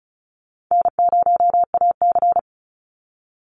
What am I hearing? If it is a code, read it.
Morse code, heard as a keyed tone, about 34 wpm, 700 Hz. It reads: N0AC
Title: Easy Call Signs Repeated 2x - 35wpm